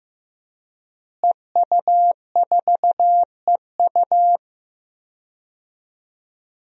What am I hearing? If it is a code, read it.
Morse code, heard as a keyed tone, about 15 wpm, 700 Hz. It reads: EU4EU
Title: DXCC-100 Call Signs - 5 Characters - Repeated 2x - 15wpm